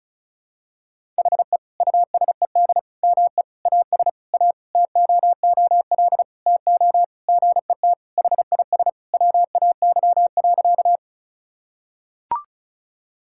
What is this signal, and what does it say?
Morse code, heard as a keyed tone, about 36 wpm, 700 Hz. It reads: HE USED ME AS A TOOL TO GET HIS WAY.
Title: Sentences from Top 1000 Words - Review Part 6 - 35wpm